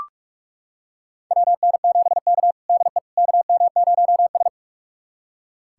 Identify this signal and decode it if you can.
Morse code, heard as a keyed tone, about 44 wpm, 690 Hz. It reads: WN7K DE KM0S
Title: USA Callsign DE Callsign - 4 Characters - 45wpm